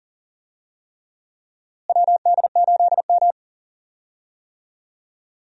Morse code, heard as a keyed tone, about 40 wpm, 700 Hz. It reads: WD8M
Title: USA Call Signs - 4 Characters - Repeated 3x - 40wpm